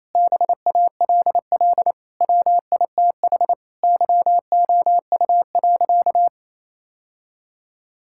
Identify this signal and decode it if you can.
Morse code, heard as a keyed tone, about 28 wpm, 700 Hz. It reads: BALL WITH YOU.
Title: Sentences from Top 1000 Words - Review Part 6 - 28wpm